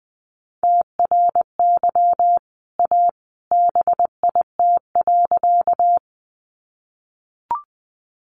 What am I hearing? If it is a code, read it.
Morse code, heard as a keyed tone, about 20 wpm, 700 Hz. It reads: TRY A BIT.
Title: Sentences from Top 900 Words 20wpm